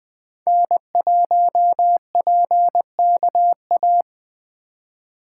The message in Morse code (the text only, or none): N1PKA